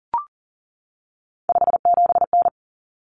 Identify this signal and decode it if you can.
Morse code, heard as a keyed tone, about 40 wpm, 700 Hz. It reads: 57N
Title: Common Cut Numbers - 3-Digit 40wpm